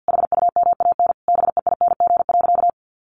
Morse code, heard as a keyed tone, about 50 wpm, 700 Hz. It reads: HUMAN BEING.
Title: Sentences from Top 1500 Words 50wpm